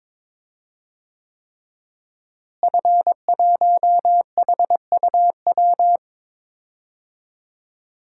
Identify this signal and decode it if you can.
Morse code, heard as a keyed tone, about 22 wpm, 700 Hz. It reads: F1HUW
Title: DXCC-100 Call Signs - 5 Characters - Repeated 3x - 22wpm